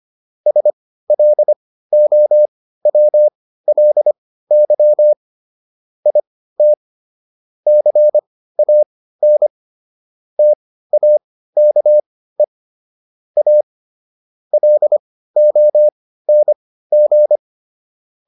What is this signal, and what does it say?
Morse code, heard as a keyed tone, about 25 wpm, spaced as for 15 wpm, 600 Hz. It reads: SLOWLY IT CAN TAKE A LONG